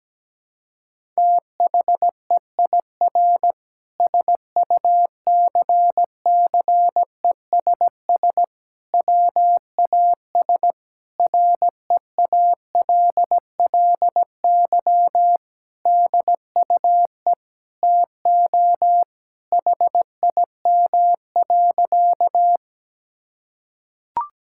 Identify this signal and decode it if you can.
Morse code, heard as a keyed tone, about 17 wpm, 700 Hz. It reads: THEIR SUCCESS WAS REALLY DUE TO HIM.
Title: Sentences from Top 1000 Words - Review Part 6 - 17wpm